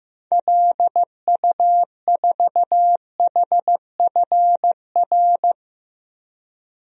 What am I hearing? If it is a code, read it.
Morse code, heard as a keyed tone, about 15 wpm, 700 Hz. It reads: LU4HFR
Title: DXCC-100 Call Signs - 6 Characters - 15wpm